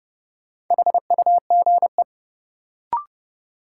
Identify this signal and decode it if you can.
Morse code, heard as a keyed tone, about 30 wpm, 700 Hz. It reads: HUGE
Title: Sentences from Top 1100 Words - New Word Drill - 30wpm